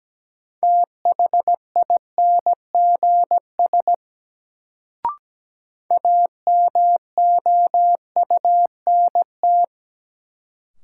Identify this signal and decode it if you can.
Morse code, heard as a keyed tone, about 17 wpm, 700 Hz. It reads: THINGS AMOUNT
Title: Sets of 3 Words - New Word Drill - 17wpm